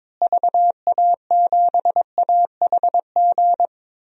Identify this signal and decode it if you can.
Morse code, heard as a keyed tone, about 22 wpm, 700 Hz. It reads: VA7AHG
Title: DXCC-100 Call Signs - 6 Characters - Repeated 2x - 22wpm